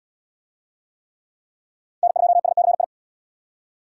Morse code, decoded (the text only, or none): N2SCI